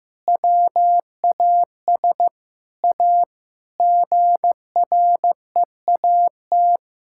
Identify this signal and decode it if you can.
Morse code, heard as a keyed tone, about 15 wpm, 700 Hz. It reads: WAS A GREAT
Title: Sentences from Top 1000 Words - Review Part 5 - 15wpm